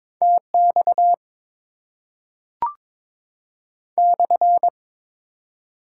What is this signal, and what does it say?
Morse code, heard as a keyed tone, about 22 wpm, 690 Hz. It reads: TX /
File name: Top 90 QSO Elements - 4x Character Spacing - 22wpm